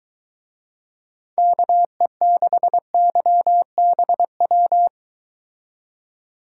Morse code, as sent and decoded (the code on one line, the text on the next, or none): -.- . -.... -.-- -... .--
KE6YBW